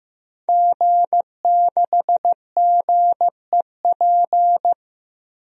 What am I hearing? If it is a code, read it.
Morse code, heard as a keyed tone, about 15 wpm, 700 Hz. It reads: G6GEP
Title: DXCC-100 Call Signs - 5 Characters - 15wpm